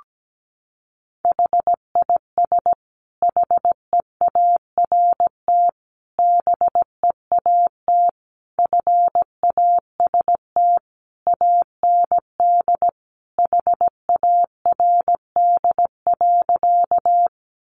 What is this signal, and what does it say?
Morse code, heard as a keyed tone, about 17 wpm, 700 Hz. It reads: HIS HEART BEAT FAST AND HARD.
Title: Sentences from Top 1300 Words 17wpm